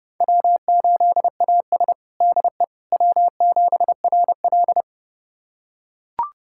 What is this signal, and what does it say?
Morse code, heard as a keyed tone, about 30 wpm, 700 Hz. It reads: W8AS DE W7RL
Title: USA Callsign DE Callsign - 4 Characters - 30wpm